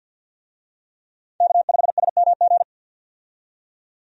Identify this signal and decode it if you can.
Morse code, heard as a keyed tone, about 49 wpm, 700 Hz. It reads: K5SMG
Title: USA Call Signs - 5 Characters - 50wpm